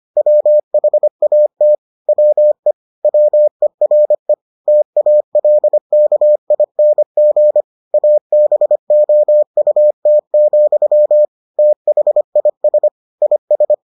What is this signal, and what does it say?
Morse code, heard as a keyed tone, about 25 wpm, 600 Hz. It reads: WHAT WE WERE TALKING ABOUT, THIS IS